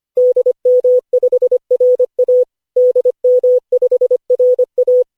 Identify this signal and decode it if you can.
Morse code, heard as a keyed tone, about 25 wpm, 500 Hz. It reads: DM5RA DM5RA
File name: cw / Session5 cw training